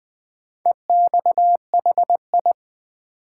Morse code, heard as a keyed tone, about 20 wpm, 700 Hz. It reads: EXHI
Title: Build A Word - 4 Letters At A Time - 10-Letter Words - 20wpm